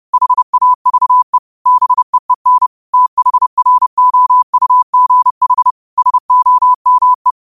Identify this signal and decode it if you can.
Morse code, heard as a keyed tone, about 30 wpm, 1000 Hz. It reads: HAVE BEEN THROUGH SOME